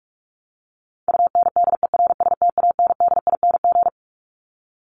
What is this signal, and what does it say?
Morse code, heard as a keyed tone, about 44 wpm, 700 Hz. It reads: UNDERSTANDING
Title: ING Suffix - Top 100 Words with Base Words 45wpm